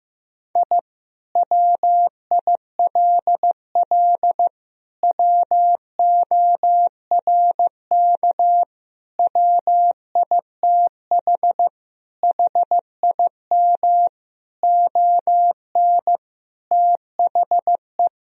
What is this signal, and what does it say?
Morse code, heard as a keyed tone, about 15 wpm, 700 Hz. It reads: I WILL WORK WITH HIM ON THE
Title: Sentences from Top 1000 Words - Review Part 4 - 15wpm